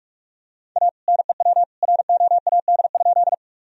Morse code, heard as a keyed tone, about 41 wpm, 700 Hz. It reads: A NEW ROAD?
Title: Sentences from Top 1000 Words - Review Part 2 - 45wpm